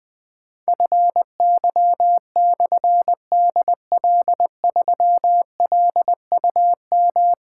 Morse code, heard as a keyed tone, about 21 wpm, 700 Hz. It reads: FY/DL3LUM
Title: Difficult Call Signs 20wpm